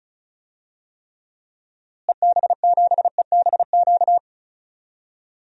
Morse code, heard as a keyed tone, about 36 wpm, 700 Hz. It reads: EB7EBQ
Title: DXCC-100 Call Signs - 6 Characters - Repeated 3x - 35wpm